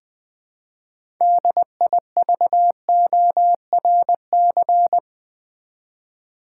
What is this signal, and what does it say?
Morse code, heard as a keyed tone, about 20 wpm, 700 Hz. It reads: DIVORC
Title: Build A Word - 3 Letters At A Time - 8-Letter Words - 20wpm